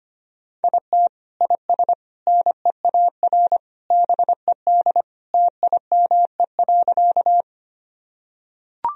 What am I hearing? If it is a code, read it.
Morse code, heard as a keyed tone, about 25 wpm, 700 Hz. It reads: IT IS NEAR BED TIME.
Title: Sentences from Top 1000 Words - Review Part 1 - 25wpm